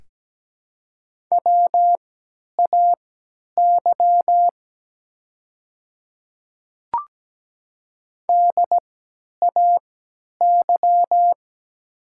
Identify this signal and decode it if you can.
Morse code, heard as a keyed tone, about 17 wpm, spaced as for 10 wpm, 700 Hz. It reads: WAY DAY
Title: 3x Character Spacing - Top 100 Words - 17wpm